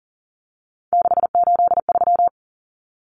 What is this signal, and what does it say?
Morse code, heard as a keyed tone, about 40 wpm, 700 Hz. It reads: 683
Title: Numbers - 3-Digit 40wpm